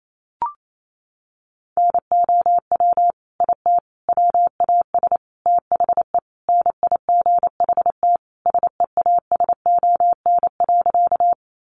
Morse code, heard as a keyed tone, about 28 wpm, 700 Hz. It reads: NOW IT WAS THE NIGHT SEASON.
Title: Sentences from Top 1400 Words 28wpm